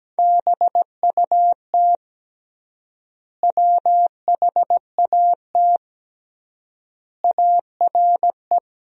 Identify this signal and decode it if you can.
Morse code, heard as a keyed tone, about 17 wpm, 700 Hz. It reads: BUT WHAT ARE
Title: 3x Word-Spacing - Sentences from Top 100 Words - 17wpm